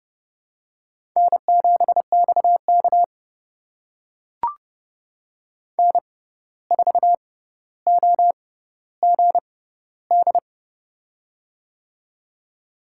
Morse code, heard as a keyed tone, about 30 wpm, spaced as for 14 wpm, 700 Hz. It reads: N7XK N4OGD
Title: USA Call Signs - 6x Character Spacing - 30wpm